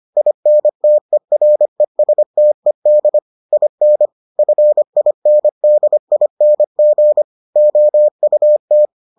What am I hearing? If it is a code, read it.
Morse code, heard as a keyed tone, about 25 wpm, 600 Hz. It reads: INTERESTED IN FINDING OUT